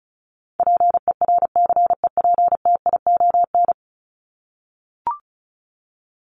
Morse code, heard as a keyed tone, about 34 wpm, 700 Hz. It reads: PERCEPTION
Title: Top 1751-2000 Words 35wpm